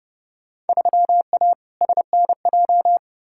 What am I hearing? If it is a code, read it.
Morse code, heard as a keyed tone, about 32 wpm, 700 Hz. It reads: 3A SNJ
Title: Field Day Exchanges - 30wpm